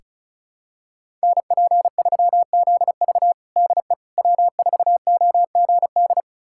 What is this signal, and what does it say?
Morse code, heard as a keyed tone, about 34 wpm, 700 Hz. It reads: NP3ZV DE W4OGD
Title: USA Callsign DE Callsign - 5 Characters - 35wpm